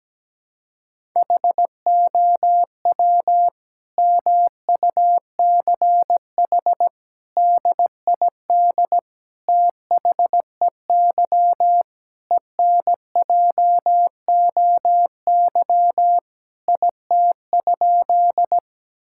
Morse code, heard as a keyed tone, about 17 wpm, 700 Hz. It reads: HOW MUCH DID THEY ENJOY IT?
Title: Sentences from Top 1000 Words - Review Part 5 - 17wpm